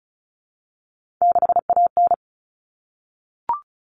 Morse code, heard as a keyed tone, about 34 wpm, 700 Hz. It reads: 6AN
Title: Common Cut Numbers - 3-Digit 35wpm